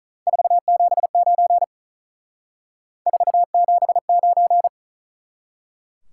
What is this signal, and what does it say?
Morse code, heard as a keyed tone, about 46 wpm, 700 Hz. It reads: 479 479